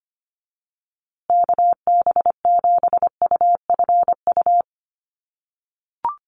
Morse code, heard as a keyed tone, about 25 wpm, 700 Hz. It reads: KB7UFU
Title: USA Call Signs Repeated 3x 25wpm